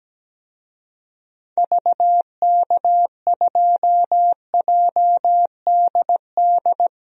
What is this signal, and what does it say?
Morse code, heard as a keyed tone, about 17 wpm, 700 Hz. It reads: VK2JDD
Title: DXCC-100 Call Signs - 6 Characters - Repeated 2x - 17wpm